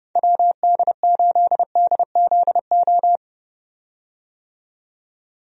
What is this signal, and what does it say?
Morse code, heard as a keyed tone, about 30 wpm, 700 Hz. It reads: WD8DZO
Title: USA Call Signs - 6 Characters - 30wpm